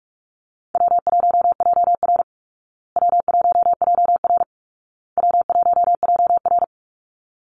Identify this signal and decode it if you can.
Morse code, heard as a keyed tone, about 44 wpm, 700 Hz. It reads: W1JR W1JR W1JR
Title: Call Signs Repeated 2x - 45wpm